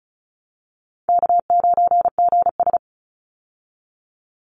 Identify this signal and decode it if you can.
Morse code, heard as a keyed tone, about 34 wpm, 700 Hz. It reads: K9GS
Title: Call Signs Repeated 2x - 35wpm